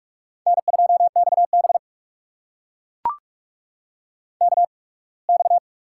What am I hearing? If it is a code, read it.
Morse code, heard as a keyed tone, about 44 wpm, spaced as for 27 wpm, 690 Hz. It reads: N2XB KX
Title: USA Call Signs - 4 Characters - 8x Character Spacing - 45wpm